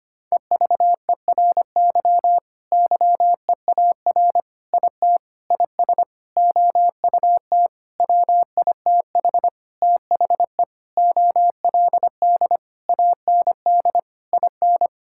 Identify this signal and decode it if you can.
Morse code, heard as a keyed tone, about 25 wpm, 700 Hz. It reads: EVERY YEAR IT IS OUT WITH THE OLD AND IN
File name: practice / 25wpm / Sentences with 9-Word Phrases - 25wpm